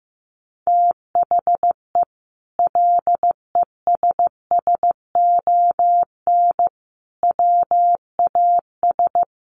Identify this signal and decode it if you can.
Morse code, heard as a keyed tone, about 15 wpm, 700 Hz. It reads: THE LESSON WAS
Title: Sentences from Top 1200 Words 15wpm